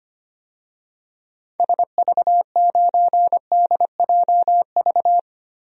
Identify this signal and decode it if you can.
Morse code, heard as a keyed tone, about 25 wpm, 700 Hz. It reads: SV9DJV
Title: DXCC-100 Call Signs - 6 Characters - Repeated 2x - 25wpm